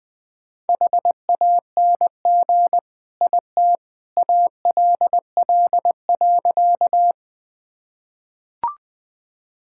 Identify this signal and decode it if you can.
Morse code, heard as a keyed tone, about 20 wpm, 700 Hz. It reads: HANG IT ALL.
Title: Sentences from Top 1600 Words 20wpm